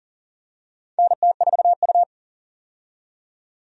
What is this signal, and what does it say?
Morse code, heard as a keyed tone, about 40 wpm, 700 Hz. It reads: NT4U
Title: USA Call Signs - 4 Characters - 40wpm